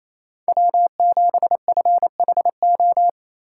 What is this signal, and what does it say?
Morse code, heard as a keyed tone, about 28 wpm, 700 Hz. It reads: W7FHO
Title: USA Call Signs - 5 Characters - 6x Character Spacing - 28wpm